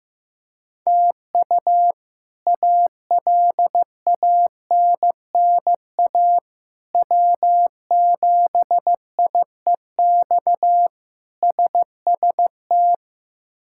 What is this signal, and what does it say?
Morse code, heard as a keyed tone, about 15 wpm, 700 Hz. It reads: TU ALANNA W7IEX SST
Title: K1USN SST QSO Practice - 15wpm